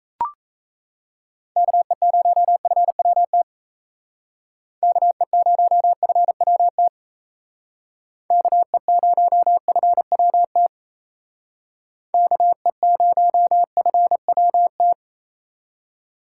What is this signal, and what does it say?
Morse code, heard as a keyed tone, about 33 wpm, 700 Hz. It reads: KE0FWT KE0FWT KE0FWT KE0FWT
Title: Speed Racing - USA 2x3 Call Signs 28wpm